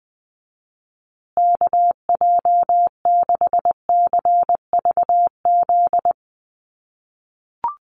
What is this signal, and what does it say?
Morse code, heard as a keyed tone, about 20 wpm, 700 Hz. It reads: KJ6CVZ
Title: USA Call Signs Repeated 2x 20wpm